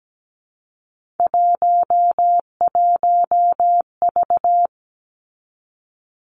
Morse code, heard as a keyed tone, about 17 wpm, 700 Hz. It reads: 11V
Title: Three Challenging Letter-Number Combinations 17wpm